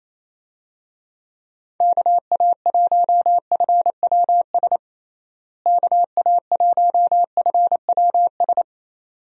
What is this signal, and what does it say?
Morse code, heard as a keyed tone, about 28 wpm, 700 Hz. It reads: KA1FWS KA1FWS
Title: USA Call Signs - 6 Characters - Repeated 2x - 28wpm